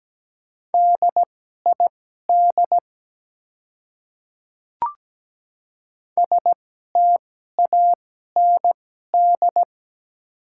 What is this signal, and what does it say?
Morse code, heard as a keyed tone, about 17 wpm, spaced as for 12 wpm, 700 Hz. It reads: DID STAND
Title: 2x Character Spacing - Top 300 Words - 17wpm